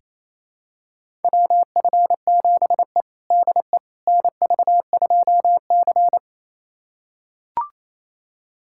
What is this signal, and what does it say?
Morse code, heard as a keyed tone, about 28 wpm, 700 Hz. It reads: WF7E DE NV2C
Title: USA Callsign DE Callsign - 4 Characters - 28wpm